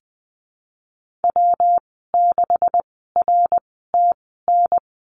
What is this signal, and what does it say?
Morse code, heard as a keyed tone, about 20 wpm, spaced as for 15 wpm, 700 Hz. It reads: W6RTN
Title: 128 Lesson 21g - Number 6 - Call Signs 20wpm